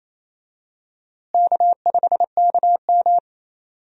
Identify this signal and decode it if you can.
Morse code, heard as a keyed tone, about 28 wpm, 700 Hz. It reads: K5KM
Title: USA Call Signs - 4 Characters - Repeated 3x - 28wpm